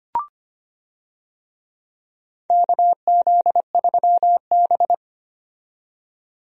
Morse code, heard as a keyed tone, about 25 wpm, 700 Hz. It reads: KZ3B
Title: USA Call Signs - 4 Characters - 25wpm